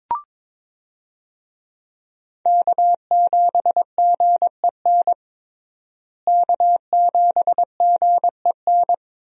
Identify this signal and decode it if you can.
Morse code, heard as a keyed tone, about 22 wpm, 700 Hz. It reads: K7GEN K7GEN
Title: USA Call Signs - 5 Characters - Repeated 3x - 22wpm